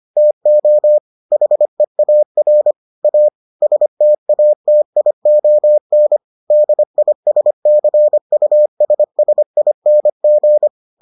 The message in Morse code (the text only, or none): TO HEAR A STATION DISCUSSING